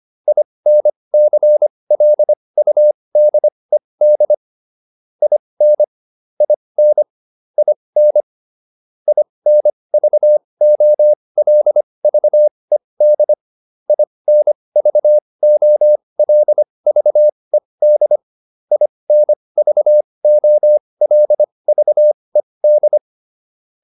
Morse code, heard as a keyed tone, about 25 wpm, spaced as for 20 wpm, 600 Hz. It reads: INCLUDED IN IN IN INVOLVED INVOLVED INVOLVED
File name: IN8 20